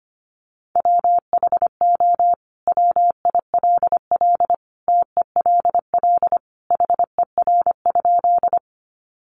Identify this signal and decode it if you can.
Morse code, heard as a keyed tone, about 25 wpm, 700 Hz. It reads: WHO WILL TELL HER?
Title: Sentences from Top 200 Words 25wpm